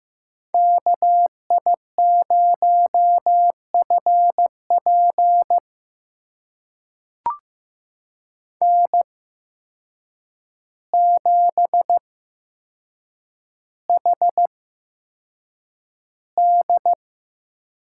Morse code, heard as a keyed tone, about 15 wpm, spaced as for 7 wpm, 700 Hz. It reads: KI0FP N7HD
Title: USA Call Signs - 8x Character Spacing - 15wpm